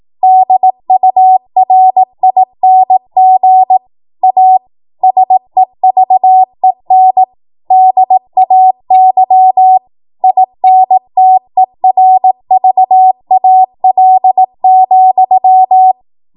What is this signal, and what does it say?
Morse code, heard as a keyed tone, about 18 wpm, 750 Hz. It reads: DURING A SEVEN DAY INTERVAL,
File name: Morse / 18WPM